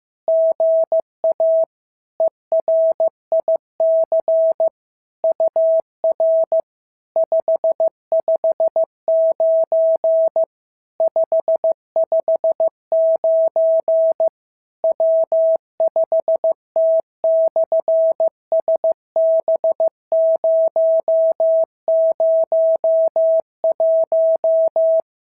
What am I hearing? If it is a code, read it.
Morse code, heard as a keyed tone, about 15 wpm, 650 Hz. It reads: GA ERIC UR 559 559 W5T/SB001
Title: SOTA QSO Practice 15wpm